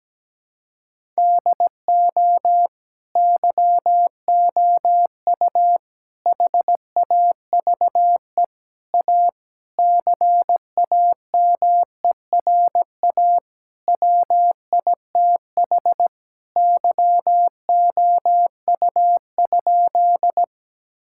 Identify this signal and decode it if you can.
Morse code, heard as a keyed tone, about 17 wpm, 700 Hz. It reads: DO YOU HAVE A CAMERA WITH YOU?
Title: Sentences from Top 1000 Words - Review Part 5 - 17wpm